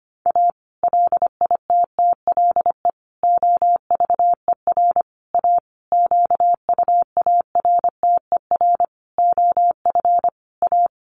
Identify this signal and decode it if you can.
Morse code, heard as a keyed tone, about 25 wpm, 700 Hz. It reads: A LITTLE OVER A QUARTER OF A
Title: Sentences from Top 1300 Words 25wpm